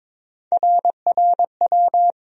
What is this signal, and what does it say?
Morse code, heard as a keyed tone, about 22 wpm, 700 Hz. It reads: RRW
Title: USA Call Sign Postfixes - 3 Letters 22wpm